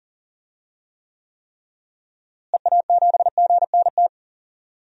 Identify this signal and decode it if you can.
Morse code, heard as a keyed tone, about 40 wpm, 700 Hz. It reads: EA7GNT
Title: DXCC-100 Call Signs - 6 Characters - Repeated 3x - 40wpm